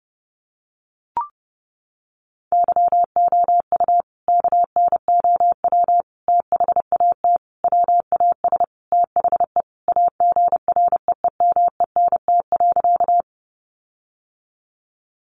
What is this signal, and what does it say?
Morse code, heard as a keyed tone, about 28 wpm, 700 Hz. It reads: YOU KNOW THAT WAS THE AGREEMENT.
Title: Sentences from Top 1500 Words 30wpm